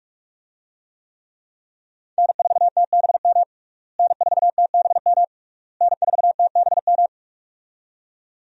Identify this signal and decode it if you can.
Morse code, heard as a keyed tone, about 46 wpm, 700 Hz. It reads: N4TBM N4TBM N4TBM
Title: USA Call Signs - 5 Characters - Repeated 3x - 45wpm